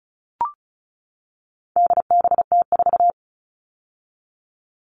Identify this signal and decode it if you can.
Morse code, heard as a keyed tone, about 34 wpm, 700 Hz. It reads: DBT4
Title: All Cut Numbers - 4-Digit 35wpm